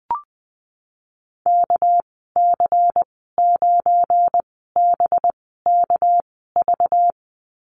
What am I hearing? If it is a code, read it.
Morse code, heard as a keyed tone, about 19 wpm, spaced as for 14 wpm, 700 Hz. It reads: KC9BKV
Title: 231 Lesson 37g - Letter x - Call Signs 20wpm